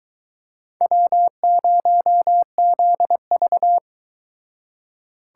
Morse code, heard as a keyed tone, about 22 wpm, 700 Hz. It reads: W0ZV